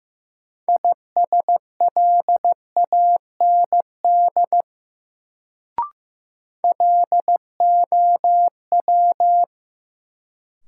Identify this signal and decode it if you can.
Morse code, heard as a keyed tone, about 15 wpm, 700 Hz. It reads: ISLAND LOW
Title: Sentences from Top 500 Words - New Word Drill - 15wpm